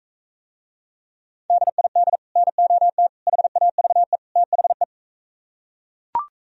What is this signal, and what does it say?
Morse code, heard as a keyed tone, about 40 wpm, 700 Hz. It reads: DID NOT HAVE THE